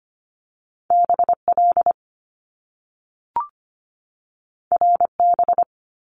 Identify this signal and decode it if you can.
Morse code, heard as a keyed tone, about 25 wpm, 700 Hz. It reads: BL RB
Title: Top Two Letter Combinations in Words 25wpm